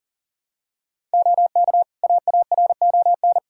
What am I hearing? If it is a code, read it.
Morse code, heard as a keyed tone, about 40 wpm, 700 Hz. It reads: OK AARON
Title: Practice QSO - Exchange 2 - Info - 40wpm